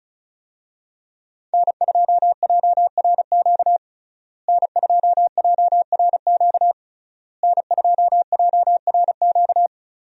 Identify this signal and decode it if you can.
Morse code, heard as a keyed tone, about 36 wpm, 700 Hz. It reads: N2JRQ N2JRQ N2JRQ
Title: USA Call Signs - 5 Characters - Repeated 3x - 35wpm